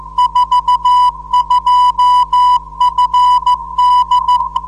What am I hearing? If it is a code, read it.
Morse code, heard as a keyed tone, about 14 wpm, 990 Hz. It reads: 42FD
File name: 472khz qso web / 29march2014 / 01 dl4yhf 29mar2014 2248utc